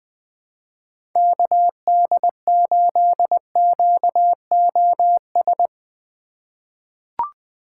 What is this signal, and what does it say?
Morse code, heard as a keyed tone, about 20 wpm, 700 Hz. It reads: KD8QOS
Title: USA Call Signs - 6 Characters - 3x Character Spacing - 20wpm